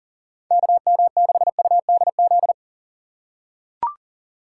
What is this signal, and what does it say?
Morse code, heard as a keyed tone, about 40 wpm, 700 Hz. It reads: KM6UDZ
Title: USA Call Signs - 4x Character Spacing - 40wpm